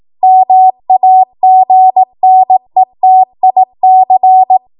Morse code, heard as a keyed tone, about 18 wpm, 750 Hz. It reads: MAGNETIC
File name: Archive / 18 WPM / 18WPM